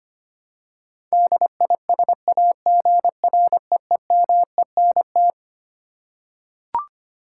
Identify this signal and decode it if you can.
Morse code, heard as a keyed tone, about 25 wpm, 700 Hz. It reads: DISAGREEMENT
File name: Twelve Letter Words - 25wpm